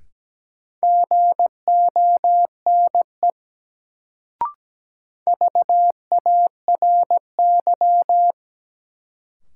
Morse code, heard as a keyed tone, about 17 wpm, 700 Hz. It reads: GONE VARY
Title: Sentences from Top 1200 Words - New Word Drill - 17wpm